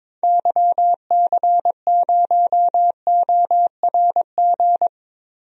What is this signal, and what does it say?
Morse code, heard as a keyed tone, about 22 wpm, 700 Hz. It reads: YC0ORG
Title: DXCC-100 Call Signs - 6 Characters - Repeated 2x - 22wpm